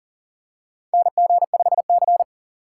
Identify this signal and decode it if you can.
Morse code, heard as a keyed tone, about 40 wpm, 690 Hz. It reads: NG5C
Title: USA Call Signs - 4 Characters - 2x Character Spacing - 40wpm